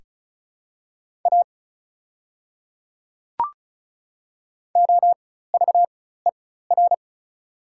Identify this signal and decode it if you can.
Morse code, heard as a keyed tone, about 36 wpm, spaced as for 16 wpm, 700 Hz. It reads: A OVER
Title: 4x Character Spacing - Top 100 Words - 35wpm